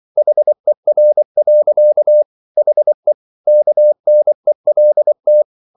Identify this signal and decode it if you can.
Morse code, heard as a keyed tone, about 24 wpm, 600 Hz. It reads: HER. HE KNELT